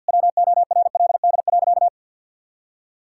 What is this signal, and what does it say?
Morse code, heard as a keyed tone, about 50 wpm, 700 Hz. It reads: WORLD.
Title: Sentences from Top 1600 Words 50wpm